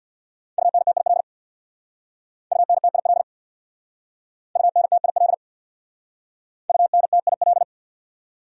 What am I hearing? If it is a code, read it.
Morse code, heard as a keyed tone, about 65 wpm, 700 Hz. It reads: UNTIL UNTIL UNTIL UNTIL